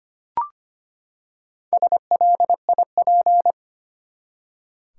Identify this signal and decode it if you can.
Morse code, heard as a keyed tone, about 25 wpm, 700 Hz. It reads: SLIP
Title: Four Letter Words - 25wpm